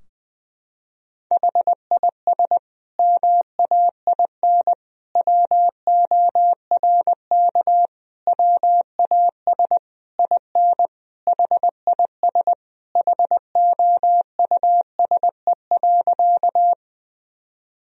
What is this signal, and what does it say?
Morse code, heard as a keyed tone, about 21 wpm, 700 Hz. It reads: HIS MAIN WORK WAS IN HIS HOUSE.
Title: Sentences from Top 1000 Words - Review Part 2 - 20wpm